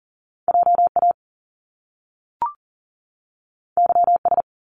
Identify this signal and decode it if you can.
Morse code, heard as a keyed tone, about 39 wpm, 700 Hz. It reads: JA YS
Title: Top Two Letter Combinations in Words 40wpm